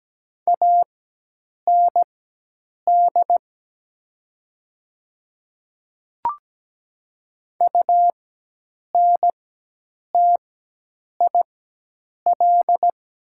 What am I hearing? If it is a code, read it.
Morse code, heard as a keyed tone, about 17 wpm, spaced as for 8 wpm, 700 Hz. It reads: AND UNTIL
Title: 4x Character Spacing - Top 500 Words - 17wpm